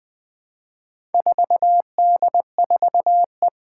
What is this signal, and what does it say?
Morse code, heard as a keyed tone, about 20 wpm, 700 Hz. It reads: 4D4E
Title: Four Challenging Letter-Number Combinations 20wpm